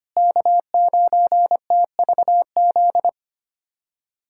Morse code, heard as a keyed tone, about 25 wpm, 700 Hz. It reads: K9TVZ
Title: USA Callsign DE Callsign - 5 Characters - 25wpm